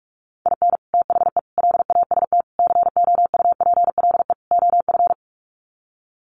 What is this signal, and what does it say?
Morse code, heard as a keyed tone, about 44 wpm, 700 Hz. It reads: IN THE LAST COUPLE OF
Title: Sets of 5 Words 45wpm